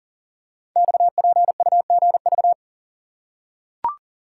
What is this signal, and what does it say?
Morse code, heard as a keyed tone, about 38 wpm, 690 Hz. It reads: XPUGV
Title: Code Groups - 5 Letters - 40wpm